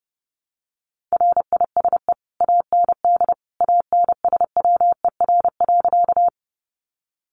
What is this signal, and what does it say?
Morse code, heard as a keyed tone, about 30 wpm, 700 Hz. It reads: RISE AND ANSWER.
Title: Sentences from Top 900 Words 30wpm